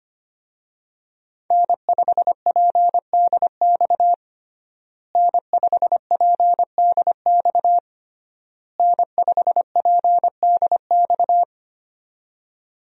Morse code, heard as a keyed tone, about 25 wpm, 700 Hz. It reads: N5PDX N5PDX N5PDX
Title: USA Call Signs - 5 Characters - Repeated 3x - 25wpm